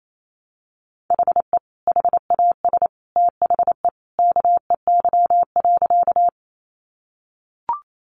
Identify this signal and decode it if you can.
Morse code, heard as a keyed tone, about 28 wpm, 700 Hz. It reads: HE HAS THE KEY.
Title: Sentences from Top 600 Words 28wpm